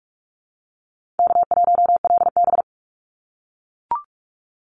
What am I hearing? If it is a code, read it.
Morse code, heard as a keyed tone, about 44 wpm, 700 Hz. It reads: K1LB
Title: NA SOTA Chaser Callsigns 45wpm